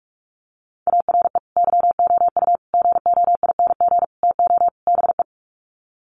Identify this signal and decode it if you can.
Morse code, heard as a keyed tone, about 44 wpm, 700 Hz. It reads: ARE YOU GOING TO BE
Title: Sets of 5 Words Encore 45wpm